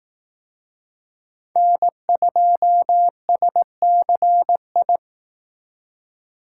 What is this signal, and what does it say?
Morse code, heard as a keyed tone, about 18 wpm, 700 Hz. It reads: N2SCI
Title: Speed Racing - USA 1x3 Call Signs 15wpm